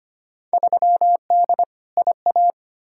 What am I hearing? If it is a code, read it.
Morse code, heard as a keyed tone, about 25 wpm, 700 Hz. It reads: 3D IA
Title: Field Day Exchanges Encore - 25wpm